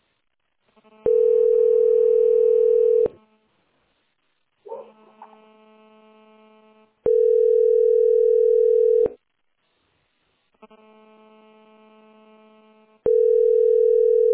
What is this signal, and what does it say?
Ringback tone (North America), 440 + 480 Hz.